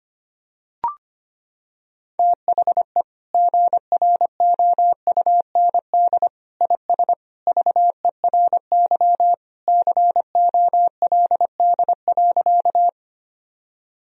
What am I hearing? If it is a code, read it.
Morse code, heard as a keyed tone, about 25 wpm, 700 Hz. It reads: THE GROUND IS VERY COLD.
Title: Sentences from Top 1000 Words - Review Part 2 - 25wpm